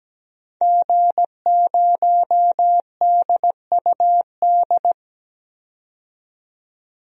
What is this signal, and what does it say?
Morse code, heard as a keyed tone, about 17 wpm, 700 Hz. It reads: G0DUD
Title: DXCC-100 Call Signs - 5 Characters - 17wpm